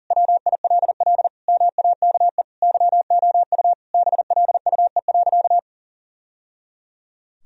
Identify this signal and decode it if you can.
Morse code, heard as a keyed tone, about 37 wpm, 700 Hz. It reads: WILL MAKE YOU BLUE.
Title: Sentences from Top 1000 Words - Review Part 3 - 40wpm